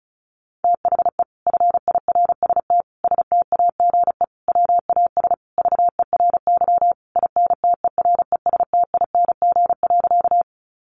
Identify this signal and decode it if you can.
Morse code, heard as a keyed tone, about 34 wpm, 700 Hz. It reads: THE FIRST STAGE WAS VERY INTERESTING.
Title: Sentences from Top 1200 Words 35wpm